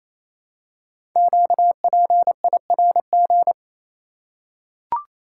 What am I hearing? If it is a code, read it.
Morse code, heard as a keyed tone, about 29 wpm, 700 Hz. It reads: QPIRG
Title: Code Groups - 5 Letters - 28wpm